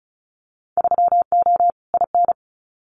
Morse code, heard as a keyed tone, about 36 wpm, 700 Hz. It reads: 3O IN
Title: Winter Field Day Exchanges Encore 35wpm